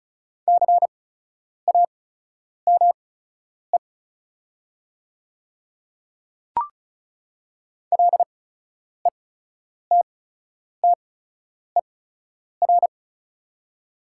Morse code, heard as a keyed tone, about 36 wpm, spaced as for 10 wpm, 690 Hz. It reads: CAME LETTER
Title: 8x Character Spacing - Top 300 Words - 35wpm